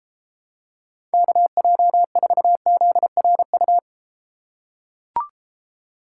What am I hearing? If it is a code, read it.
Morse code, heard as a keyed tone, about 33 wpm, 700 Hz. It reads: KJ4ZRU